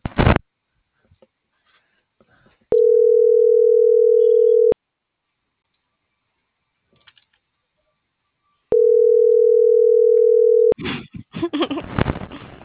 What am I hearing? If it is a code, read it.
Ringback tone (North America), 440 + 480 Hz.